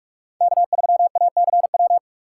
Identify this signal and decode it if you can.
Morse code, heard as a keyed tone, about 44 wpm, 700 Hz. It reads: K3ACW